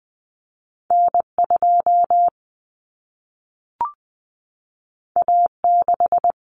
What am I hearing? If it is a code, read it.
Morse code, heard as a keyed tone, about 20 wpm, 700 Hz. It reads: N2 A6
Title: Common Cut Numbers - 2-Digit 20wpm